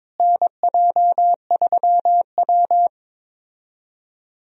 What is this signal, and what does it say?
Morse code, heard as a keyed tone, about 22 wpm, 700 Hz. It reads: NJ3W
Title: USA Call Signs - 4 Characters - Repeated 2x - 22wpm